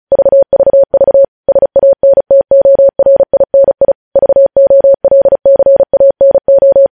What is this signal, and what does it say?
Morse code, heard as a keyed tone, about 36 wpm, 550 Hz. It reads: VVV SANTORINI VOLCANO
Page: News Headlines in Morse Code at 35 WPM